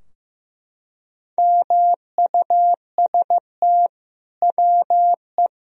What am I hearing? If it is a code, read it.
Morse code, heard as a keyed tone, about 16 wpm, 700 Hz. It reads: MUST WE
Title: Sentences from Top 1000 Words - Review Part 3 - 15wpm